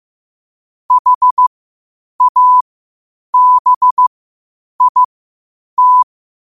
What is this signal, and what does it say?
Morse code, heard as a keyed tone, about 15 wpm, spaced as for 8 wpm, 1000 Hz. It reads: HABIT
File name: qotdfiles / 8wpm